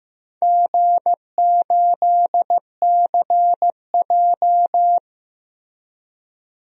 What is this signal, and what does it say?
Morse code, heard as a keyed tone, about 15 wpm, 700 Hz. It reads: G8CJ
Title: DXCC-100 Call Signs - 4 Characters - Repeated 3x - 15wpm